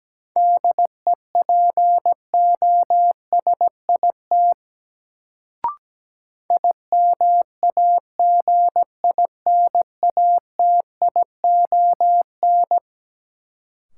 Morse code, heard as a keyed tone, about 17 wpm, 700 Hz. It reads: DEPOSIT IMAGINATION
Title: Sentences from Top 1300 Words - New Word Drill - 17wpm